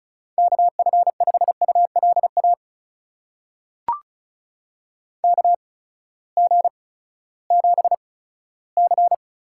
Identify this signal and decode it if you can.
Morse code, heard as a keyed tone, about 36 wpm, spaced as for 18 wpm, 700 Hz. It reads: KF5ULA KG7C
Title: USA Call Signs - 6 Characters - 8x Character Spacing - 35wpm